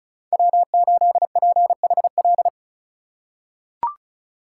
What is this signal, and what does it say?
Morse code, heard as a keyed tone, about 36 wpm, 700 Hz. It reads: W8PHL
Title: USA Call Signs - 5 Characters - 4x Character Spacing - 35wpm